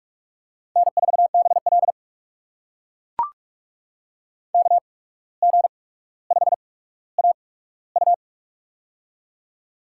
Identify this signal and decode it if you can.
Morse code, heard as a keyed tone, about 44 wpm, spaced as for 18 wpm, 700 Hz. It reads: N4BL KG5AU
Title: USA Call Signs - 8x Character Spacing - 45wpm